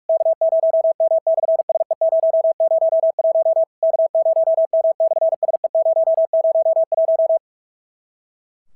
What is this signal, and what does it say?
Morse code, heard as a keyed tone, about 46 wpm, 650 Hz. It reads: K0M/SE001 K0M/SE001
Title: SOTA QSO Practice 45wpm